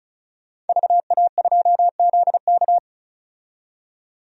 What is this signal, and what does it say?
Morse code, heard as a keyed tone, about 36 wpm, 700 Hz. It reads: VA2ZK
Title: DXCC-100 Call Signs - 5 Characters - 35wpm